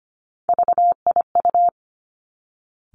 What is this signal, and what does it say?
Morse code, heard as a keyed tone, about 25 wpm, 700 Hz. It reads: VIU
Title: Three Challenging Letter-Number Combinations 25wpm